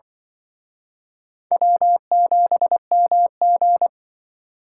Morse code, heard as a keyed tone, about 25 wpm, 700 Hz. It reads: W7MG